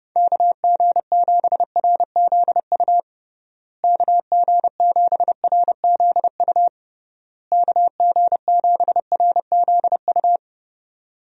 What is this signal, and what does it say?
Morse code, heard as a keyed tone, about 30 wpm, 700 Hz. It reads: KG7RZU KG7RZU KG7RZU
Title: USA Call Signs - 6 Characters - Repeated 3x - 30wpm